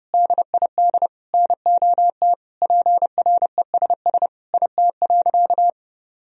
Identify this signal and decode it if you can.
Morse code, heard as a keyed tone, about 30 wpm, 700 Hz. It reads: DID NOT PRESS IT.
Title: Sentences from Top 1000 Words - Review Part 1 - 30wpm